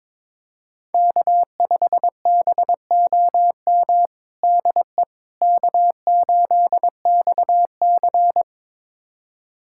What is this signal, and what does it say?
Morse code, heard as a keyed tone, about 22 wpm, 700 Hz. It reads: K5BOM DE K8XC
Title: USA Callsign DE Callsign 22wpm